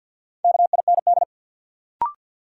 Morse code, heard as a keyed tone, about 53 wpm, 700 Hz. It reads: KIND
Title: Sentences from Top 200 Words - New Word Drill - 50wpm